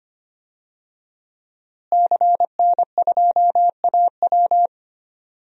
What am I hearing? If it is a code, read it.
Morse code, heard as a keyed tone, about 25 wpm, 700 Hz. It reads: CN2AW
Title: DXCC-100 Call Signs - 5 Characters - Repeated 2x - 25wpm